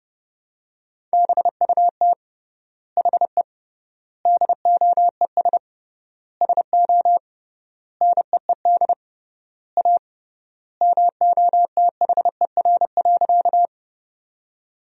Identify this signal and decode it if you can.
Morse code, heard as a keyed tone, about 29 wpm, 700 Hz. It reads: BUT HE DOES SO NEED A MOTHER.
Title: 3x Word-Spacing - Sentences from Top 200 Words - 30wpm